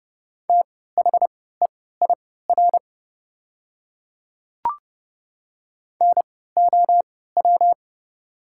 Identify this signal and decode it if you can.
Morse code, heard as a keyed tone, about 30 wpm, spaced as for 17 wpm, 700 Hz. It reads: THEIR NOW
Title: 3x Character Spacing - Top 100 Words - 30wpm